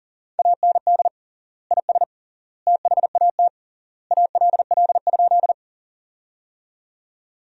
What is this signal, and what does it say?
Morse code, heard as a keyed tone, about 37 wpm, 700 Hz. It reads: AND IS THAT ALL?
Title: 3x Word-Spacing - Sentences from Top 100 Words - 40wpm